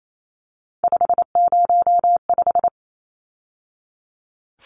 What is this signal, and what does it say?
Morse code, heard as a keyed tone, about 28 wpm, 700 Hz. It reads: 505
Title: Numbers - 3-Digit 28wpm